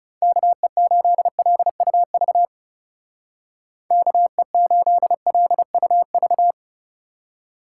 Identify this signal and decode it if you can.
Morse code, heard as a keyed tone, about 36 wpm, 700 Hz. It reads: KE8LUV KE8LUV
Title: Speed Racing - USA 2x3 Call Signs 30wpm